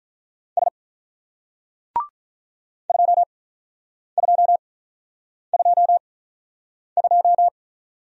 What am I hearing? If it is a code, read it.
Morse code, heard as a keyed tone, about 46 wpm, spaced as for 9 wpm, 700 Hz. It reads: S 2222